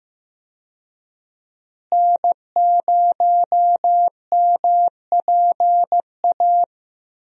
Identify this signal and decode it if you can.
Morse code, heard as a keyed tone, about 15 wpm, 700 Hz. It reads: N0MPA
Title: USA Call Signs - 5 Characters - Repeated 2x - 15wpm